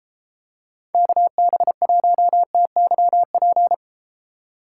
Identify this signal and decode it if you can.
Morse code, heard as a keyed tone, about 32 wpm, 700 Hz. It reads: KB1TYP